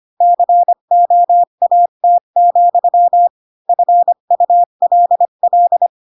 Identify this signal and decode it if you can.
Morse code, heard as a keyed tone, about 25 wpm, 700 Hz. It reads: COAT, FULL
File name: Pinocchio / Pinocchio ch8 20 25 ik1hge